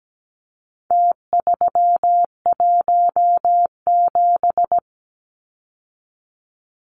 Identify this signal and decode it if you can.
Morse code, heard as a keyed tone, about 17 wpm, 700 Hz. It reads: T317
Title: Numbers - Contest Style - 4-Digit 17wpm